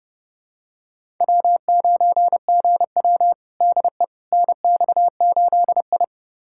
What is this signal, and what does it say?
Morse code, heard as a keyed tone, about 29 wpm, 700 Hz. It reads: W9GW DE NX8I
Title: USA Callsign DE Callsign - 4 Characters - 30wpm